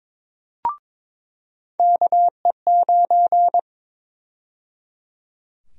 Morse code, heard as a keyed tone, about 22 wpm, 700 Hz. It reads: KE9
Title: USA Call Sign Prefixes - 22wpm